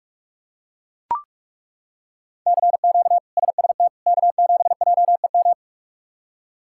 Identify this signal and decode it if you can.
Morse code, heard as a keyed tone, about 41 wpm, 700 Hz. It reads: CQ SST K7JEM
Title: K1USN SST QSO Practice - 45wpm